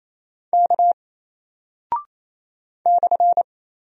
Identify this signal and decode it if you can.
Morse code, heard as a keyed tone, about 27 wpm, 700 Hz. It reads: K /
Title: Single Character (Letter:Number:Punctuation:Prosign) 28wpm